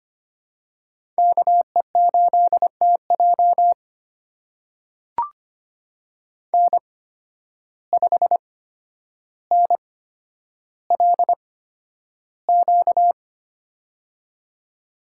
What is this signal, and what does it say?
Morse code, heard as a keyed tone, about 25 wpm, spaced as for 11 wpm, 700 Hz. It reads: KE8TJ N5NLQ
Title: USA Call Signs - 5 Characters - 8x Character Spacing - 25wpm